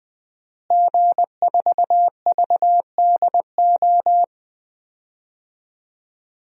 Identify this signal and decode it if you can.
Morse code, heard as a keyed tone, about 20 wpm, 700 Hz. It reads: G4VDO
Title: DXCC-100 Call Signs - 5 Characters - Repeated 3x - 20wpm